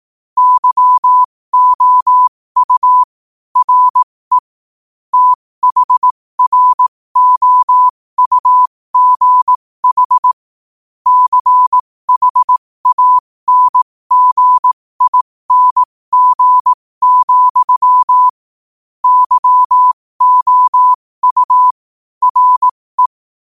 Morse code, heard as a keyed tone, about 18 wpm, spaced as for 15 wpm, 1000 Hz. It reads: YOU RE THROUGH CHANGING, YOU RE
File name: qotdfiles / 15wpm